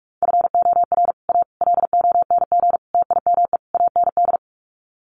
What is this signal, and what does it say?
Morse code, heard as a keyed tone, about 44 wpm, 700 Hz. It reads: FOR A LONG TIME AND
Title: Sets of 5 Words Encore2 45wpm